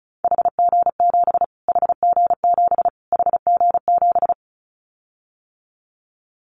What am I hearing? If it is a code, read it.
Morse code, heard as a keyed tone, about 34 wpm, 700 Hz. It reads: HG7 HG7 HG7
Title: Callsign Prefixes for Top 25 DXCC - Repeated 3x - 35wpm